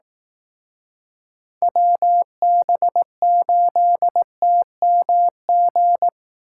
Morse code, heard as a keyed tone, about 19 wpm, 700 Hz. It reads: WB8TMG